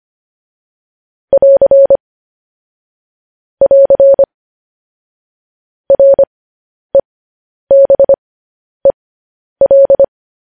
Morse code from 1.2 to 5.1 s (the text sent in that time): <AR> <AR>